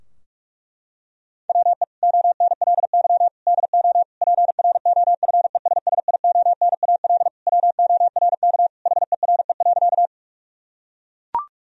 Morse code, heard as a keyed tone, about 41 wpm, 700 Hz. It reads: WE ONLY DO PROFESSIONAL WORK HERE.
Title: Sentences from Top 1000 Words - Review Part 4 - 45wpm